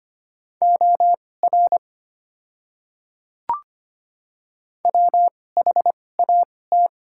Morse code, heard as a keyed tone, about 25 wpm, spaced as for 18 wpm, 700 Hz. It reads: OR WHAT
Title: 2x Character Spacing - Top 300 Words - 25wpm